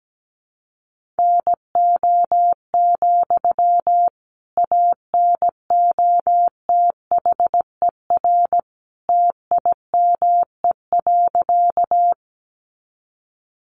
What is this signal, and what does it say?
Morse code, heard as a keyed tone, about 17 wpm, 700 Hz. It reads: NO, ANOTHER TIME.
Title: Sentences from Top 600 Words 17wpm